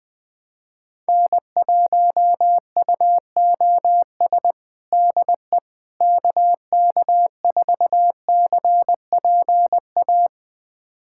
Morse code, heard as a keyed tone, about 19 wpm, 700 Hz. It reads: N1UOS DE KK4CPA
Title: USA Callsign DE Callsign 20wpm